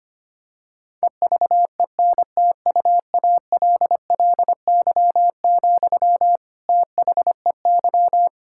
Morse code, heard as a keyed tone, about 25 wpm, 700 Hz. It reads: EVENTUALLY, THEY
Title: Sentences from Top 1000 Words - Review Part 6 - 25wpm